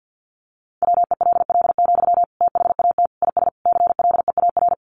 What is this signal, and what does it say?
Morse code, heard as a keyed tone, about 50 wpm, 700 Hz. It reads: WELL, THAT IS CLEAR
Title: Sentences from Top 900 Words 50wpm